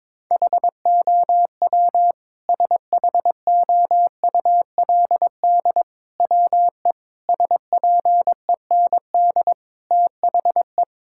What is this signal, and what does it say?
Morse code, heard as a keyed tone, about 22 wpm, 700 Hz. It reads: HOW SHOULD WE SPEND THE
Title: Sentences from Top 1000 Words - Review Part 5 - 22wpm